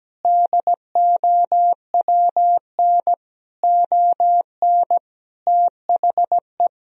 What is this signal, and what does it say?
Morse code, heard as a keyed tone, about 17 wpm, 700 Hz. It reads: DOWN ON THE
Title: Sentences from Top 1000 Words - Review Part 3 - 17wpm